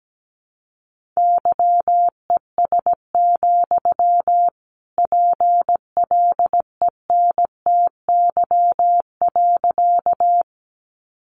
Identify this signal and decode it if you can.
Morse code, heard as a keyed tone, about 17 wpm, 700 Hz. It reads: YES, PLENTY.
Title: Sentences from Top 1300 Words 17wpm